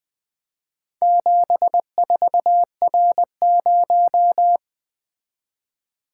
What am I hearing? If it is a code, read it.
Morse code, heard as a keyed tone, about 20 wpm, 700 Hz. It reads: 74R0
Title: Four Letter-Number Combinations - 20wpm